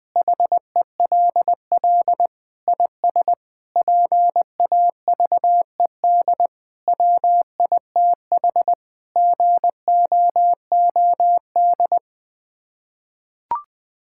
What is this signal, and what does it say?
Morse code, heard as a keyed tone, about 20 wpm, 700 Hz. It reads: HELL IS PAVED WITH GOOD
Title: Sets of 8 Words - 20wpm